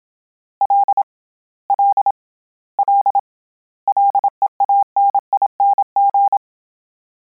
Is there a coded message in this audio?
Yes: Morse code from 0.6 to 6.4 s, heard as a keyed tone, about 27 wpm, 800 Hz. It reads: L L L LEANING